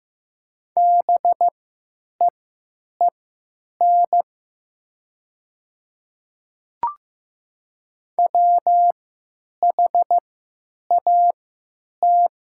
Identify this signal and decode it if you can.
Morse code, heard as a keyed tone, about 15 wpm, spaced as for 9 wpm, 700 Hz. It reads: BEEN WHAT
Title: 3x Character Spacing - Top 200 Words - 15wpm